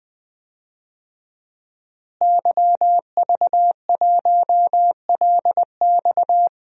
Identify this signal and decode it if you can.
Morse code, heard as a keyed tone, about 21 wpm, 700 Hz. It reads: YV1LX
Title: DXCC-100 Call Signs - 5 Characters - Repeated 2x - 20wpm